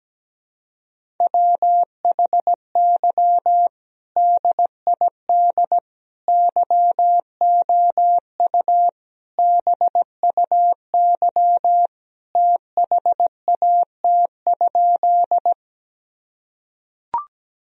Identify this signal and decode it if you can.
Morse code, heard as a keyed tone, about 17 wpm, 700 Hz. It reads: WHY DID YOU BUY THAT?
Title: Sentences from Top 1000 Words - Review Part 4 - 17wpm